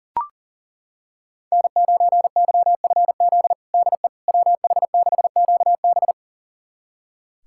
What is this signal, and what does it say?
Morse code, heard as a keyed tone, about 40 wpm, 700 Hz. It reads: N9YFZ DE WH6QB
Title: USA Callsign DE Callsign - 5 Characters - 40wpm